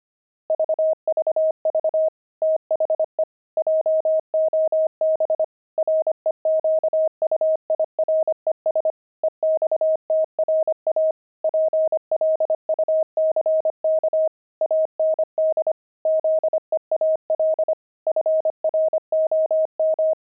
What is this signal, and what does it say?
Morse code, heard as a keyed tone, about 25 wpm, 630 Hz. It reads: VVV THE JOB REQUIRES EXTRA PLUCK AND ZEAL FROM
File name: pangram-fs-25-cs-25-zeal